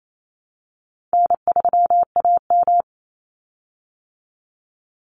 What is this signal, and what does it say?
Morse code, heard as a keyed tone, about 28 wpm, 700 Hz. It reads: N3AM
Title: Easy Call Signs Repeated 3x - 28wpm